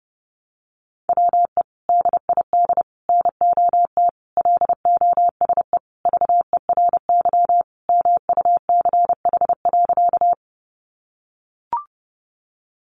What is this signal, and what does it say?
Morse code, heard as a keyed tone, about 30 wpm, 700 Hz. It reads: WE DID NOT LOSE VERY MUCH.
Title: Sentences from Top 1100 Words 30wpm